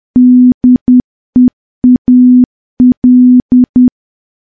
Morse code, heard as a keyed tone, about 10 wpm, 250 Hz. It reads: DEAL
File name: morse / morse-training-10WPM